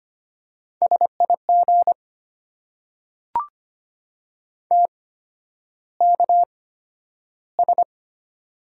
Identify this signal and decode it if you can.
Morse code, heard as a keyed tone, about 25 wpm, spaced as for 11 wpm, 690 Hz. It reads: SIG TKS
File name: Top 90 QSO Elements - 8x Character Spacing - 25wpm